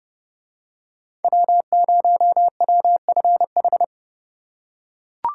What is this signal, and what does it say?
Morse code, heard as a keyed tone, about 28 wpm, 700 Hz. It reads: W0WFH